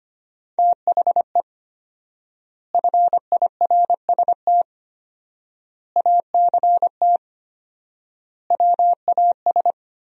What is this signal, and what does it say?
Morse code, heard as a keyed tone, about 25 wpm, 700 Hz. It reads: THE FIRST ACT WAS
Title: 4x Word-Spacing - Sentences from Top 200 Words - 25wpm